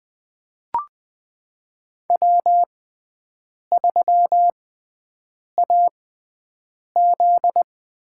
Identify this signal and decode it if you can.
Morse code, heard as a keyed tone, about 20 wpm, spaced as for 7 wpm, 700 Hz. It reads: W3AZ
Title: USA Call Signs - 5 Characters - 6x Character Spacing - 20wpm